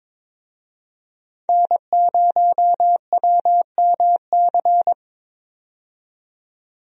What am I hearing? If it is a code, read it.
Morse code, heard as a keyed tone, about 22 wpm, 700 Hz. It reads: N0WMC
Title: USA Call Signs - 5 Characters - 22wpm